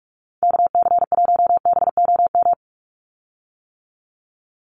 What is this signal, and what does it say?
Morse code, heard as a keyed tone, about 44 wpm, 700 Hz. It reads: KC1BOM
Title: USA Call Signs 45wpm